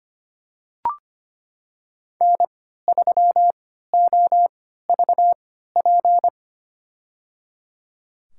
Morse code, heard as a keyed tone, about 25 wpm, spaced as for 14 wpm, 700 Hz. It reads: N3OVP
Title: USA Call Signs - 5 Characters - 3x Character Spacing - 25wpm